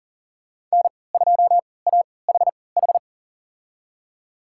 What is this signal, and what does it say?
Morse code, heard as a keyed tone, about 40 wpm, spaced as for 23 wpm, 700 Hz. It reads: N2AHH
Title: USA Call Signs - 5 Characters - 3x Character Spacing - 40wpm